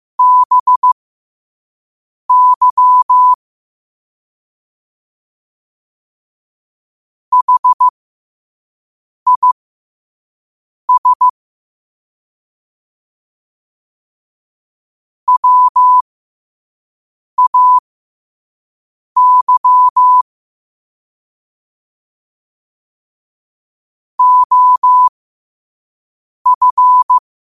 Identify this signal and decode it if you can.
Morse code, heard as a keyed tone, about 15 wpm, spaced as for 5 wpm, 1000 Hz. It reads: BY HIS WAY OF
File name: qotdfiles / 5wpm